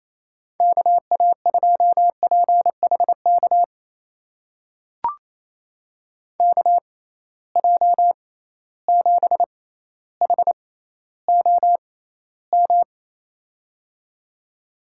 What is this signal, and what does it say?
Morse code, heard as a keyed tone, about 28 wpm, spaced as for 14 wpm, 700 Hz. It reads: KA2PHK KJ7HOM
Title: USA Call Signs - 6 Characters - 6x Character Spacing - 28wpm